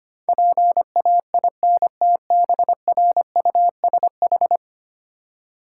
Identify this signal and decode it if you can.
Morse code, heard as a keyed tone, about 25 wpm, 700 Hz. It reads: PAINTBRUSH
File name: practice / 25wpm / Difficult Words - Speed-Racing with Hints - 25wpm